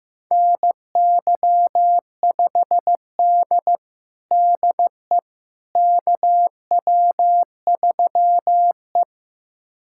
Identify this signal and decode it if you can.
Morse code, heard as a keyed tone, about 16 wpm, 700 Hz. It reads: NY5D DE KW3E
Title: USA Callsign DE Callsign - 4 Characters - 15wpm